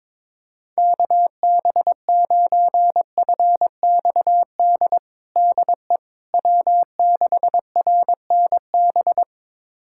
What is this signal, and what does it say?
Morse code, heard as a keyed tone, about 22 wpm, 700 Hz. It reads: KB9FXD DE W6RNB
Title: USA Callsign DE Callsign 22wpm